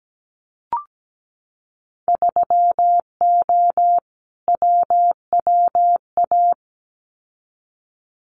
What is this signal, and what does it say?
Morse code, heard as a keyed tone, about 17 wpm, 700 Hz. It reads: 3O WWA
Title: Winter Field Day Exchanges Encore 17wpm